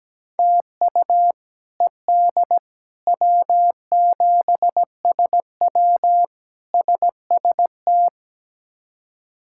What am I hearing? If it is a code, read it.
Morse code, heard as a keyed tone, about 17 wpm, 700 Hz. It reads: TU ED W7SW SST
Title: K1USN SST QSO Practice - 17wpm